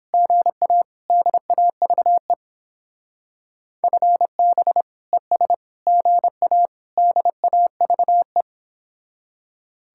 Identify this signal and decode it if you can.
Morse code, heard as a keyed tone, about 29 wpm, 700 Hz. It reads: GA DAVE FB ES GA DAVE